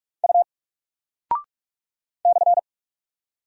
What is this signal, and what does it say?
Morse code, heard as a keyed tone, about 45 wpm, 700 Hz. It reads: U /
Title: Single Character (Letter:Number:Punctuation:Prosign) Alt 45wpm